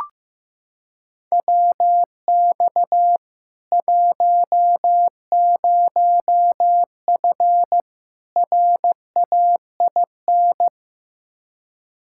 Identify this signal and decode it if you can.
Morse code, heard as a keyed tone, about 15 wpm, 700 Hz. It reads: WX 10F RAIN
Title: K1USN SST QSO Practice 15wpm